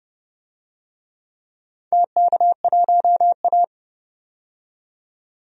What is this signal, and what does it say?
Morse code, heard as a keyed tone, about 31 wpm, 700 Hz. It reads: TK1A
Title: DXCC-100 Call Signs - 4 Characters - 30wpm